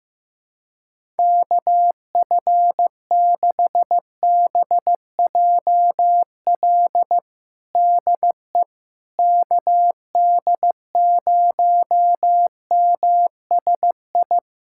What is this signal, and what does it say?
Morse code, heard as a keyed tone, about 15 wpm, 700 Hz. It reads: KF6BJL DE KD0MSI
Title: USA Callsign DE Callsign - 6 Characters - 15wpm